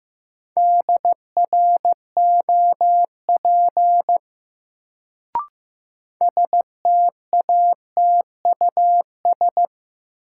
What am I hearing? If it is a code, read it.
Morse code, heard as a keyed tone, about 15 wpm, 700 Hz. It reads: DROP STATUS
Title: Sentences from Top 1400 Words - New Word Drill - 15wpm